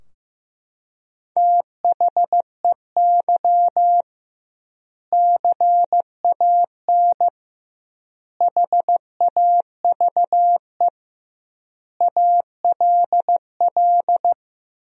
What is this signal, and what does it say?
Morse code, heard as a keyed tone, about 15 wpm, 700 Hz. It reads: THEY CAN HAVE ALL
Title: 2x Word-Spacing - Sentences from Top 400 Words - 15wpm